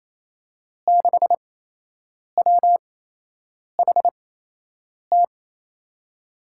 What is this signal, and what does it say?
Morse code, heard as a keyed tone, about 27 wpm, spaced as for 8 wpm, 700 Hz. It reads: 6WHT
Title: USA Call Signs - 6 Characters - 8x Character Spacing - 28wpm